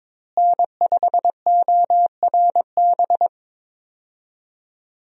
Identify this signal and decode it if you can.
Morse code, heard as a keyed tone, about 22 wpm, 700 Hz. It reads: N5ORB
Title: USA Call Signs - 5 Characters - Repeated 2x - 22wpm